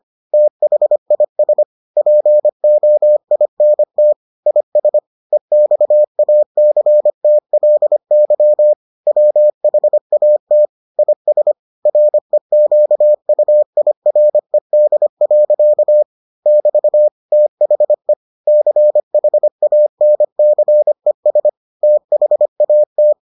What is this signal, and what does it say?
Morse code, heard as a keyed tone, about 25 wpm, 600 Hz. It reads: THIS POINT IS EXACTLY WHAT IS REQUIRED. <BT> THE CHANCES THAT